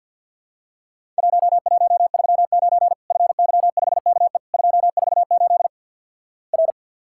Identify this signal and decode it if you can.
Morse code, heard as a keyed tone, about 50 wpm, 700 Hz. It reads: 1139 FY5KE 248 R
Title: Worked All Europe DX Contest - QTC Exchange - 50wpm